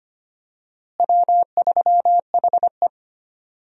Morse code, heard as a keyed tone, about 25 wpm, 700 Hz. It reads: W3HE
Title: USA Call Signs - 4 Characters - Repeated 3x - 25wpm